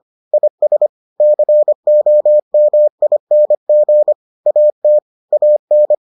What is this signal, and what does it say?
Morse code, heard as a keyed tone, about 25 wpm, 600 Hz. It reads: IS COMING AT AN